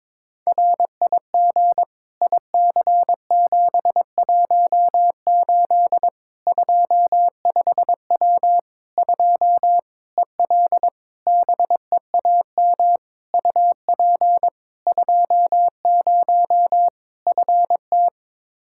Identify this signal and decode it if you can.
Morse code, heard as a keyed tone, about 22 wpm, 700 Hz. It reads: RIG IC718 25W 2 EL BEAM UP 20 FT
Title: Practice QSO - Exchange 2 - Info - 22wpm